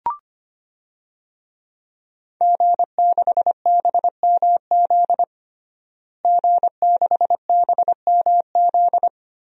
Morse code, heard as a keyed tone, about 25 wpm, 700 Hz. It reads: G6BMZ G6BMZ
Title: DXCC-100 Call Signs - 5 Characters - Repeated 3x - 25wpm